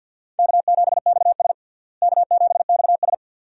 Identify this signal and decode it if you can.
Morse code, heard as a keyed tone, about 52 wpm, 700 Hz. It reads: K7XS K7XS
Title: USA Call Signs - 4 Characters - Repeated 2x - 50wpm